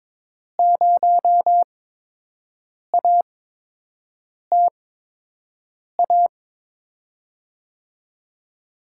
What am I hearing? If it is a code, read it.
Morse code, heard as a keyed tone, about 22 wpm, spaced as for 6 wpm, 700 Hz. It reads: 0ATA
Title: USA Call Signs - 8x Character Spacing - 22wpm